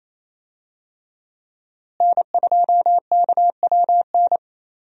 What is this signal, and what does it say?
Morse code, heard as a keyed tone, about 28 wpm, 700 Hz. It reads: N2KWN
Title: USA Call Signs - 5 Characters - Repeated 3x - 28wpm